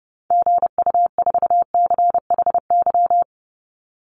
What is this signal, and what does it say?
Morse code, heard as a keyed tone, about 30 wpm, 700 Hz. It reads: GU4CHY
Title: Call Signs - 30wpm